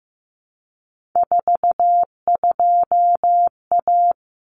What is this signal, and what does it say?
Morse code, heard as a keyed tone, about 15 wpm, 700 Hz. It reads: 42A
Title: Common Cut Numbers - 3-Digit 15wpm